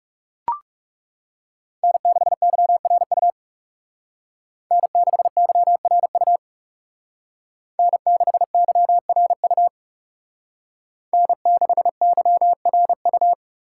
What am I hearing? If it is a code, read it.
Morse code, heard as a keyed tone, about 34 wpm, 700 Hz. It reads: N6YRU N6YRU N6YRU N6YRU
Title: Speed Racing - USA 1x3 Call Signs 30wpm